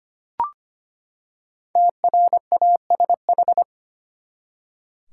Morse code, heard as a keyed tone, about 25 wpm, 700 Hz. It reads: TRASH
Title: Five Letter Words - 25wpm